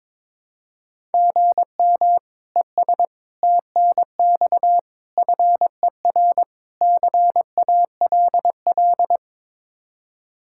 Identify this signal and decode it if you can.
Morse code, heard as a keyed tone, about 22 wpm, 700 Hz. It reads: GM ES TNX FER CALL
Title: Practice QSO - Exchange 1 - Beginning - 22wpm